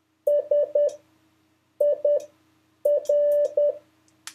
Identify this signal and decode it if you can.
Morse code, heard as a keyed tone, about 10 wpm, spaced as for 7 wpm, 580 Hz. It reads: SIR
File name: 5of5-test